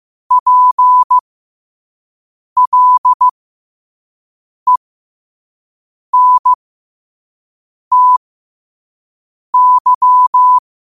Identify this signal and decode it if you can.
Morse code, heard as a keyed tone, about 15 wpm, spaced as for 5 wpm, 1000 Hz. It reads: PLENTY
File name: qotdfiles / 5wpm